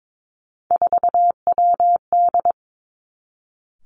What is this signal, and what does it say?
Morse code, heard as a keyed tone, about 22 wpm, 700 Hz. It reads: 4WD
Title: Three Challenging Letter-Number Combinations 22wpm